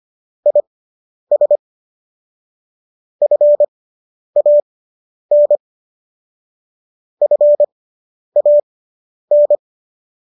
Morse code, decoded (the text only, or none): IS FAN FAN